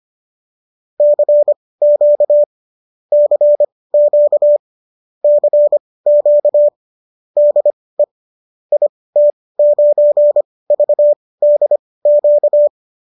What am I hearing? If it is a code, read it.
Morse code, heard as a keyed tone, about 25 wpm, spaced as for 18 wpm, 590 Hz. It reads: CQ CQ CQ DE IT9VDQ
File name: qso215 18